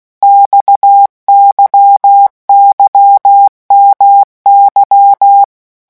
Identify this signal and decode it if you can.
Morse code, heard as a keyed tone, about 16 wpm, 800 Hz. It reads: XYYMY
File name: morse mp3 / revision5-vitesse 15 mots